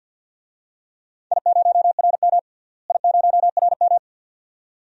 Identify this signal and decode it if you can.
Morse code, heard as a keyed tone, about 53 wpm, 700 Hz. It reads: I0RM I0RM
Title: DXCC-100 Call Signs - 4 Characters - Repeated 2x - 50wpm